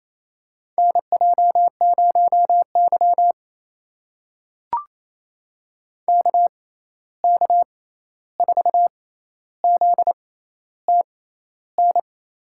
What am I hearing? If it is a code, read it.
Morse code, heard as a keyed tone, about 28 wpm, spaced as for 13 wpm, 700 Hz. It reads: NJ0Y KK4ZTN
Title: USA Call Signs - 6x Character Spacing - 28wpm